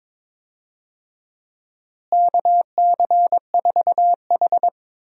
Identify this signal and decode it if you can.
Morse code, heard as a keyed tone, about 23 wpm, 700 Hz. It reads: KC4H
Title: USA Call Signs - 4 Characters - Repeated 2x - 22wpm